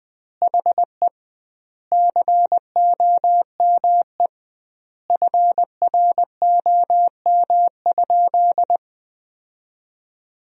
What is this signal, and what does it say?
Morse code, heard as a keyed tone, about 21 wpm, 700 Hz. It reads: HE COME FROM?
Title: 2x Word-Spacing - Sentences from Top 300 Words - 20wpm